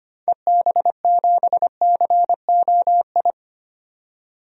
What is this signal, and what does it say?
Morse code, heard as a keyed tone, about 25 wpm, 700 Hz. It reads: EB7COI
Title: DXCC-100 Call Signs - 6 Characters - 25wpm